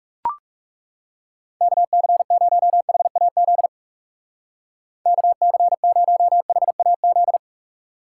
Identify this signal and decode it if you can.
Morse code, heard as a keyed tone, about 36 wpm, 700 Hz. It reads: KC0HAZ KC0HAZ
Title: Speed Racing - USA 2x3 Call Signs 30wpm